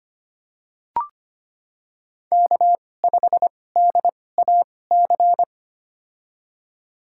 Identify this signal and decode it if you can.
Morse code, heard as a keyed tone, about 25 wpm, spaced as for 18 wpm, 700 Hz. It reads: K5DAC
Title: USA Call Signs - 5 Characters - 2x Character Spacing - 25wpm